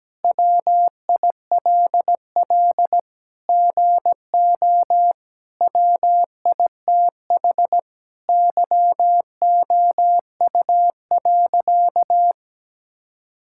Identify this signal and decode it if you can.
Morse code, heard as a keyed tone, about 17 wpm, 700 Hz. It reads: WILL GO WITH YOU.
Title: Sentences from Top 1600 Words 17wpm